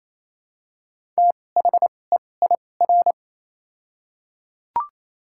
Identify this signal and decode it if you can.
Morse code, heard as a keyed tone, about 28 wpm, 700 Hz. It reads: T H E I R
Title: 2x Character Spacing - Top 200 Words - 28wpm